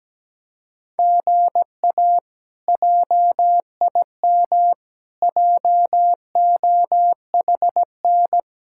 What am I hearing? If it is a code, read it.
Morse code, heard as a keyed tone, about 17 wpm, 700 Hz. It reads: GA JIM JOHN
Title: K1USN SST QSO Practice - 17wpm